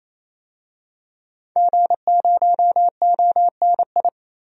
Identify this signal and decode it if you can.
Morse code, heard as a keyed tone, about 28 wpm, 700 Hz. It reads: G0ONI
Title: DXCC-100 Call Signs - 5 Characters - 28wpm